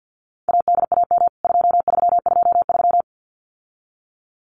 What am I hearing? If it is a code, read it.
Morse code, heard as a keyed tone, about 48 wpm, 700 Hz. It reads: ADAM 2323
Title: CWT Exchange 50wpm